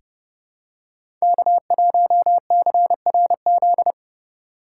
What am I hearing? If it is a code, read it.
Morse code, heard as a keyed tone, about 29 wpm, 700 Hz. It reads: K1CRZ
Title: USA Call Signs - 5 Characters - 2x Character Spacing - 30wpm